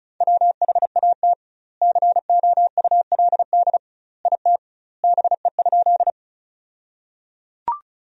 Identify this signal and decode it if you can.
Morse code, heard as a keyed tone, about 34 wpm, 700 Hz. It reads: WHAT COULD IT BE?
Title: 2x Word-Spacing - Sentences from Top 100 Words - 35wpm